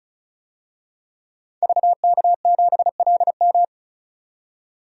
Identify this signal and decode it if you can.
Morse code, heard as a keyed tone, about 36 wpm, 700 Hz. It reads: VK7LM
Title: DXCC-100 Call Signs - 5 Characters - 35wpm